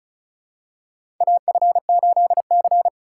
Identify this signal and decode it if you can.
Morse code, heard as a keyed tone, about 34 wpm, 700 Hz. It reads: AF8C
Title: USA Call Signs - 4 Characters - 4x Character Spacing - 35wpm